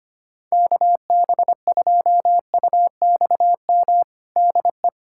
Morse code, heard as a keyed tone, about 25 wpm, 700 Hz. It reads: KB2UXM DE
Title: USA Callsign DE Callsign - 6 Characters - 25wpm